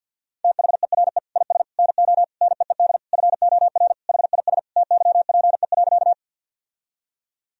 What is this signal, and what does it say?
Morse code, heard as a keyed tone, about 50 wpm, 700 Hz. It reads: THERE IS NO NEED FOR HIS TYPE.
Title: Sentences from Top 1000 Words - Review Part 3 - 50wpm